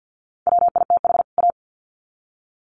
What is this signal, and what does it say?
Morse code, heard as a keyed tone, about 50 wpm, 700 Hz. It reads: WITH A
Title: Sets of 2 Words 50wpm